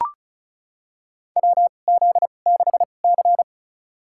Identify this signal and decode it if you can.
Morse code, heard as a keyed tone, about 35 wpm, spaced as for 25 wpm, 690 Hz. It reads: WZ6C
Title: USA Call Signs - 4 Characters - 2x Character Spacing - 35wpm